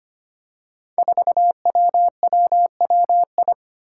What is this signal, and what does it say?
Morse code, heard as a keyed tone, about 25 wpm, 700 Hz. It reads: 4WWWI
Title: Five Challenging Letter-Number Combinations 25wpm